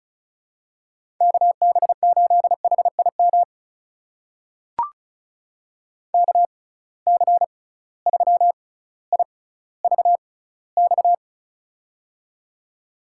Morse code, heard as a keyed tone, about 36 wpm, spaced as for 18 wpm, 700 Hz. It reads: KB8HIM KC3IVX
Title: USA Call Signs - 6 Characters - 6x Character Spacing - 35wpm